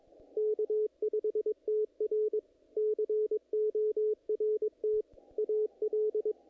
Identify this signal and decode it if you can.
Morse code, heard as a keyed tone, about 22 wpm, 430 Hz, with radio noise. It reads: K5TR CORT AL
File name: contests / naqp-cw-k5tr-op-k5pi- -2000